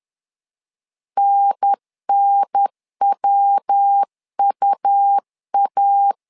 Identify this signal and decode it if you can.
Morse code, heard as a keyed tone, about 10 wpm, 800 Hz. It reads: NNWUA
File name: LEZIONI / 3 MWRO / 3 MW 10wpm 1959